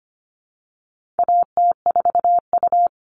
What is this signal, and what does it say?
Morse code, heard as a keyed tone, about 25 wpm, 700 Hz. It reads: AT4U
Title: All Cut Numbers - 4-Digit 25wpm